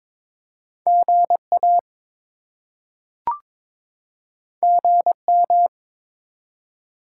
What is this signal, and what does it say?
Morse code, heard as a keyed tone, about 22 wpm, 700 Hz. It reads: GA GM
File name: practice / 22wpm / Top 20 QSO Elements - 22wpm